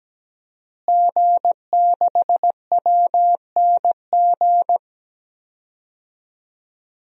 Morse code, heard as a keyed tone, about 17 wpm, 700 Hz. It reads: G6WNG
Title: DXCC-100 Call Signs - 5 Characters - Repeated 3x - 17wpm